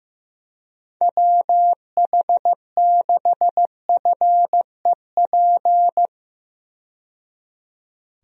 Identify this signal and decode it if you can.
Morse code, heard as a keyed tone, about 15 wpm, 700 Hz. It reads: WH6FEP
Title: USA Call Signs - 6 Characters - 15wpm